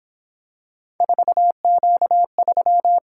Morse code, heard as a keyed tone, about 27 wpm, 700 Hz. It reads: 4Q3
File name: Speed-Racing - Three Letter-Number Combinations - 22wpm